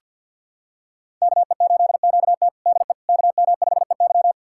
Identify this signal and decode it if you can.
Morse code, heard as a keyed tone, about 49 wpm, 700 Hz. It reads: KE8QT DE KM5EY
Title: USA Callsign DE Callsign - 5 Characters - 50wpm